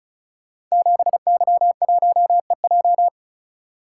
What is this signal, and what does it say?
Morse code, heard as a keyed tone, about 36 wpm, 700 Hz. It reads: 7Y1EJ
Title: Code Groups - 5 Letter:Numbers - 35wpm